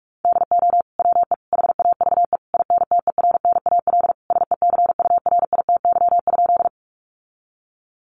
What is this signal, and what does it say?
Morse code, heard as a keyed tone, about 44 wpm, 700 Hz. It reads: DO WE HAVE INTERNAL SECURITY?
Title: Sentences from Top 1400 Words 45wpm